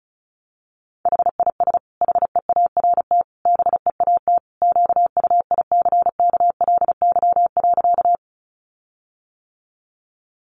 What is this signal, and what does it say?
Morse code, heard as a keyed tone, about 34 wpm, 700 Hz. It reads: HIS HEART BEAT QUICKLY.
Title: Sentences from Top 1300 Words 35wpm